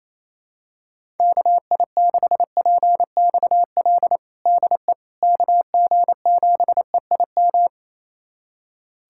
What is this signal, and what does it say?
Morse code, heard as a keyed tone, about 28 wpm, 700 Hz. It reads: KI6PXL DE KG7EIM
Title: USA Callsign DE Callsign 28wpm